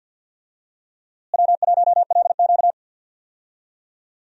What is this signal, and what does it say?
Morse code, heard as a keyed tone, about 46 wpm, 700 Hz. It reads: W1LQ
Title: USA Call Signs - 4 Characters - Repeated 3x - 50wpm